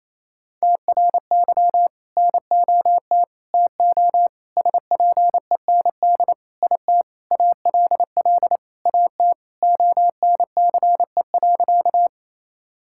Morse code, heard as a keyed tone, about 28 wpm, 700 Hz. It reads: TRY NOT TO SPEND IT ALL AT ONCE.
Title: Sentences from Top 1000 Words - Review Part 5 - 28wpm